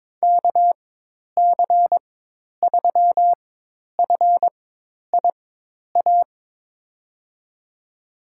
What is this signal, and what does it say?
Morse code, heard as a keyed tone, about 22 wpm, spaced as for 10 wpm, 700 Hz. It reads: KC3FIA
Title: USA Call Signs - 6 Characters - 4x Character Spacing - 22wpm